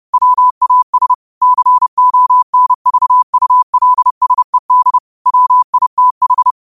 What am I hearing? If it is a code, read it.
Morse code, heard as a keyed tone, about 30 wpm, 1000 Hz. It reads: WAS CONVULSED WITH
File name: qotdfiles / 30wpm